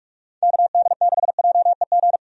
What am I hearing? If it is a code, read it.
Morse code, heard as a keyed tone, about 47 wpm, 700 Hz. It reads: KD6JEG
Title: USA Call Signs - 6 Characters - 45wpm